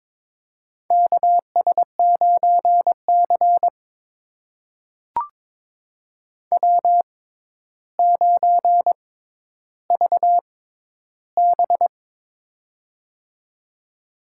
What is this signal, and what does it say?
Morse code, heard as a keyed tone, about 22 wpm, spaced as for 11 wpm, 700 Hz. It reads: KS9C W9VB
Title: USA Call Signs - 4 Characters - 6x Character Spacing - 22wpm